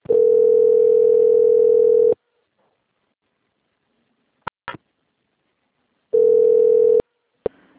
Ringback tone (North America), 440 + 480 Hz.
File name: out- - - - - .1558